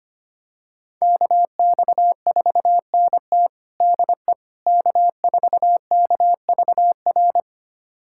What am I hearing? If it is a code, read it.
Morse code, heard as a keyed tone, about 25 wpm, 700 Hz. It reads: KX4NT DE K4KVR
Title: USA Callsign DE Callsign - 5 Characters - 25wpm